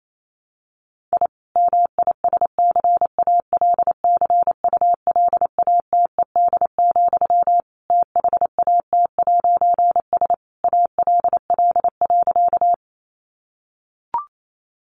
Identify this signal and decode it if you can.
Morse code, heard as a keyed tone, about 28 wpm, 700 Hz. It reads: I MISCALCULATED, THAT'S ALL.
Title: A Sound Of Thunder 28wpm